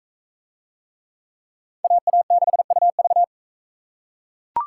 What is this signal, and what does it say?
Morse code, heard as a keyed tone, about 43 wpm, 700 Hz. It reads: AA6UV